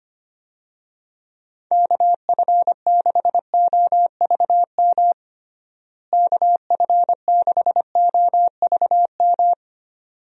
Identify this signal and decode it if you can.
Morse code, heard as a keyed tone, about 25 wpm, 700 Hz. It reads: KF6OVM KF6OVM
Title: USA Call Signs - 6 Characters - Repeated 2x - 25wpm